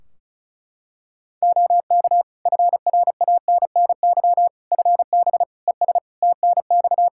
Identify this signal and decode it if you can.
Morse code, heard as a keyed tone, about 34 wpm, 700 Hz. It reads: OK FRANNY FB ES TNX
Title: Practice QSO - Exchange 2 - Info - 35wpm